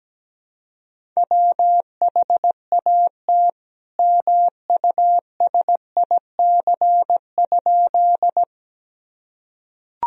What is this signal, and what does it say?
Morse code, heard as a keyed tone, about 17 wpm, 700 Hz. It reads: WHAT MUSIC?
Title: Sentences from Top 1000 Words - Review Part 1 - 17wpm